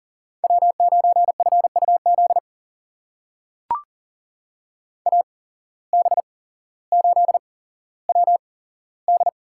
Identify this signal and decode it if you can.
Morse code, heard as a keyed tone, about 40 wpm, spaced as for 17 wpm, 700 Hz. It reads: W9FUZ AB8WD
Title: USA Call Signs - 5 Characters - 8x Character Spacing - 40wpm